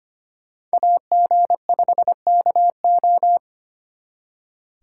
Morse code, heard as a keyed tone, about 32 wpm, 700 Hz. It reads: AG5KO